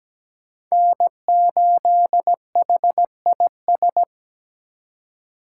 Morse code, heard as a keyed tone, about 20 wpm, 700 Hz. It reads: N8HIS
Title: Speed Racing - USA 1x3 Call Signs 17wpm